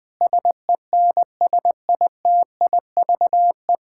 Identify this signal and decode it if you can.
Morse code, heard as a keyed tone, about 20 wpm, 700 Hz. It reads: SENSITIVE
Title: Sentences from Top 1600 Words 20wpm